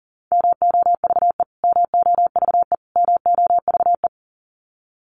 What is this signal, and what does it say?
Morse code, heard as a keyed tone, about 40 wpm, 700 Hz. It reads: MOVE MOVE MOVE
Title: Top 300 Words Repeated 40wpm